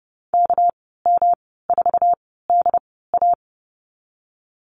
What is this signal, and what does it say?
Morse code, heard as a keyed tone, about 30 wpm, spaced as for 17 wpm, 700 Hz. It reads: KM4DA
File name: Lesson 37g - Letter x - Call Signs 30wpm